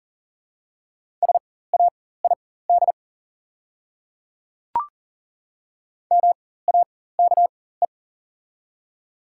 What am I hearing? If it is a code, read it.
Morse code, heard as a keyed tone, about 40 wpm, spaced as for 19 wpm, 700 Hz. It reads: SAID MAKE
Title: 4x Character Spacing - Top 100 Words - 40wpm